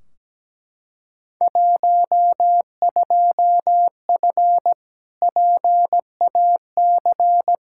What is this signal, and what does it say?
Morse code, heard as a keyed tone, about 17 wpm, 700 Hz. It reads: 12F PAC
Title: Field Day Exchanges Encore - 17wpm